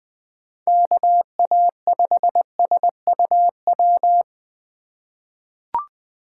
Morse code, heard as a keyed tone, about 20 wpm, 700 Hz. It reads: KA5SUW
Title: USA Call Signs - 2x Character Spacing - 20wpm